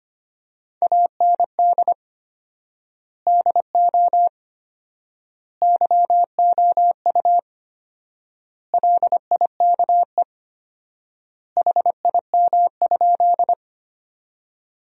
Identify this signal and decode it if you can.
Morse code, heard as a keyed tone, about 25 wpm, 700 Hz. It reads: AND DO YOU LIKE HIM?
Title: 4x Word-Spacing - Sentences from Top 100 Words - 25wpm